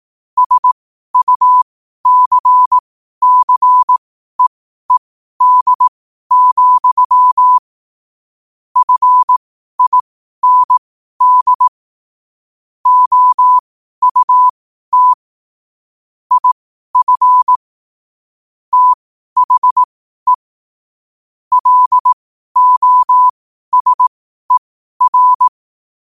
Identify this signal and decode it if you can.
Morse code, heard as a keyed tone, about 18 wpm, spaced as for 12 wpm, 1000 Hz. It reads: SUCCEED, FIND OUT IF THE LOSER
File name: qotdfiles / 12wpm